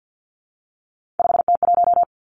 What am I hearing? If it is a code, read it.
Morse code, heard as a keyed tone, about 50 wpm, 700 Hz. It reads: 5T1
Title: Three Challenging Letter-Number Combinations 50wpm